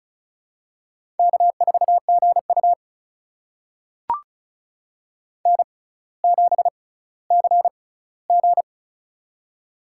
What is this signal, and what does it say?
Morse code, heard as a keyed tone, about 36 wpm, spaced as for 18 wpm, 700 Hz. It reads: K4GU N7CG
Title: USA Call Signs - 4 Characters - 6x Character Spacing - 35wpm